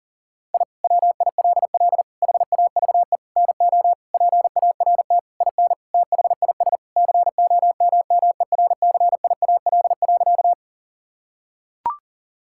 Morse code, heard as a keyed tone, about 37 wpm, 700 Hz. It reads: I WILL HAVE NO PART IN THIS COMMERCIAL.
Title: Sentences from Top 1000 Words - Review Part 5 - 40wpm